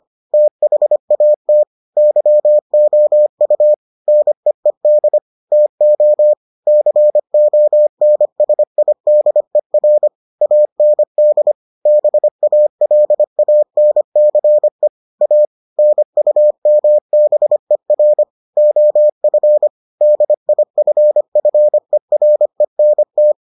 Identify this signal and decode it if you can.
Morse code, heard as a keyed tone, about 25 wpm, 600 Hz. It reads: THAT YOU NEED TO CONSIDER AND BALANCE A NUMBER OF DIFFERENT